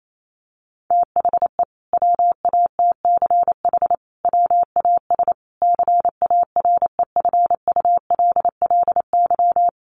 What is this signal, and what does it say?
Morse code, heard as a keyed tone, about 28 wpm, 700 Hz. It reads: THE WATCH WAS CAREFULLY
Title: Sentences from Top 1300 Words 28wpm